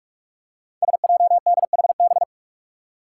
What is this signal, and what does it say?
Morse code, heard as a keyed tone, about 44 wpm, 700 Hz. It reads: SJDHB
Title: Five Challenging Letter-Number Combinations 45wpm